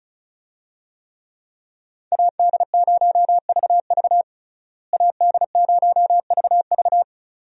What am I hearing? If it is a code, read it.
Morse code, heard as a keyed tone, about 34 wpm, 700 Hz. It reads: AD0VV AD0VV
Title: USA Call Signs - 5 Characters - Repeated 2x - 35wpm